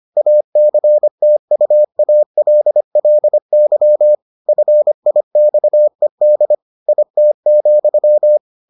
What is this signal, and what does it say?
Morse code, heard as a keyed tone, about 25 wpm, 600 Hz. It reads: ACTUALLY FIXED IT,